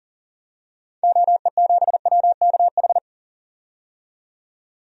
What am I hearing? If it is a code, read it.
Morse code, heard as a keyed tone, about 40 wpm, 700 Hz. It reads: OE7WKH
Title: DXCC-100 Call Signs - 6 Characters - 40wpm